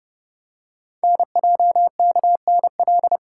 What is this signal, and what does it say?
Morse code, heard as a keyed tone, about 29 wpm, 700 Hz. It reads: NJKNL
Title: Code Groups - 5 Letter:Numbers - 30wpm